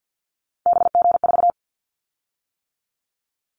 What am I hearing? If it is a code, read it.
Morse code, heard as a keyed tone, about 50 wpm, 690 Hz. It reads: BG4
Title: Callsign Prefixes for Top 100 DXCC - 50wpm